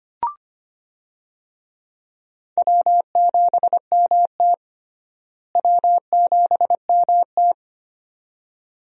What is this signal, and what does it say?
Morse code, heard as a keyed tone, about 25 wpm, 700 Hz. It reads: W7MT W7MT
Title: USA Call Signs - 4 Characters - Repeated 2x - 25wpm